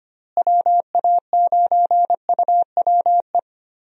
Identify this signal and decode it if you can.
Morse code, heard as a keyed tone, about 25 wpm, 700 Hz. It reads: WA9UWE
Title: USA Call Signs - 6 Characters - Repeated 3x - 25wpm